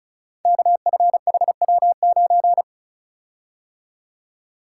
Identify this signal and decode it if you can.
Morse code, heard as a keyed tone, about 36 wpm, 700 Hz. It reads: KFHW9
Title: Code Groups - 5 Letter:Numbers - 35wpm